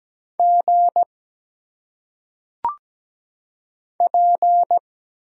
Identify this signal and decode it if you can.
Morse code, heard as a keyed tone, about 17 wpm, 700 Hz. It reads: G P
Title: Single Letters - Phonetic Pronunciation 17wpm